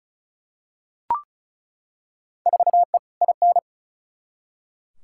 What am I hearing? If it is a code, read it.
Morse code, heard as a keyed tone, about 36 wpm, 700 Hz. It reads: 4E IN
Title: Field Day Exchanges - 35wpm